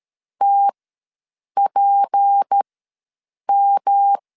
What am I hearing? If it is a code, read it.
Morse code, heard as a keyed tone, about 13 wpm, spaced as for 7 wpm, 800 Hz. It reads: TPM
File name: LEZIONI / 8 YJ45 / 8 YJ4 ws 12wpm 3249